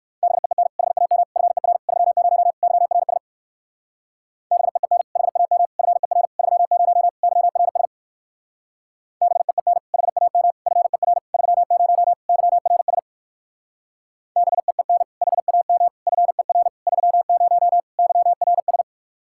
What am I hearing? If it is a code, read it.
Morse code, heard as a keyed tone, about 58 wpm, 700 Hz. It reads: BEEN HAM FER 30 YRS BEEN HAM FER 30 YRS BEEN HAM FER 30 YRS BEEN HAM FER 30 YRS